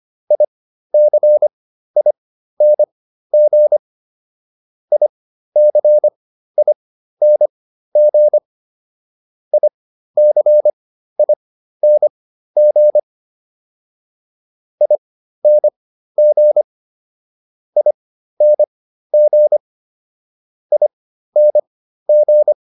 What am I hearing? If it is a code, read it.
Morse code, heard as a keyed tone, about 25 wpm, spaced as for 13 wpm, 600 Hz. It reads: ICING ICING ICING ING ING ING